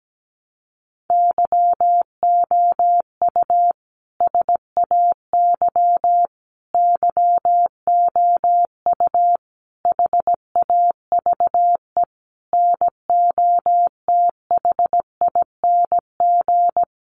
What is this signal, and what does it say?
Morse code, heard as a keyed tone, about 17 wpm, 700 Hz. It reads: YOU SAY YOU HAVE NOTHING
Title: Sentences from Top 700 Words 17wpm